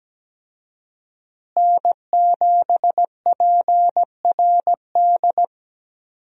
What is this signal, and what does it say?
Morse code, heard as a keyed tone, about 17 wpm, 700 Hz. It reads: N7PRD
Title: USA Call Signs - 5 Characters - Repeated 2x - 17wpm